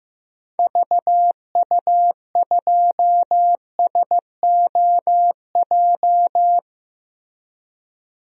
Morse code, heard as a keyed tone, about 16 wpm, 700 Hz. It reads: VU2SOJ
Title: DXCC-100 Call Signs - 6 Characters - Repeated 2x - 15wpm